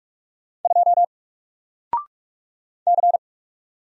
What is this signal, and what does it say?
Morse code, heard as a keyed tone, about 44 wpm, 700 Hz. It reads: 2 C
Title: Single Character (Letter:Number:Punctuation:Prosign) Alt 45wpm